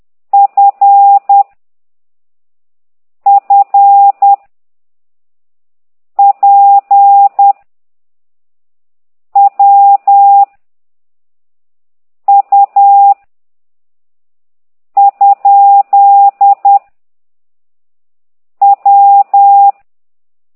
Morse code, heard as a keyed tone, about 10 wpm, 800 Hz. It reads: F F P W U ? W
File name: ct2idw / exerc9